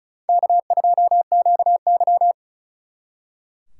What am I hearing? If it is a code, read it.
Morse code, heard as a keyed tone, about 45 wpm, 700 Hz. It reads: K2QY